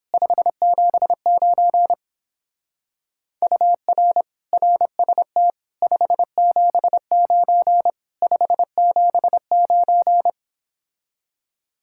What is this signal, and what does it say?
Morse code, heard as a keyed tone, about 29 wpm, 700 Hz. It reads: 579 UR RST 579 579